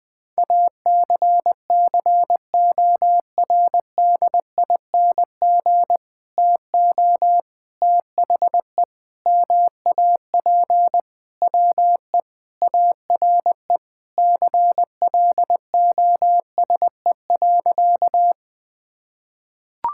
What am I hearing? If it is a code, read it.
Morse code, heard as a keyed tone, about 20 wpm, 700 Hz. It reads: ACCORDING TO THE MAP WE ARE CLOSE.
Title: Sentences with 2-Word Phrases - Spoken Beginning - 20wpm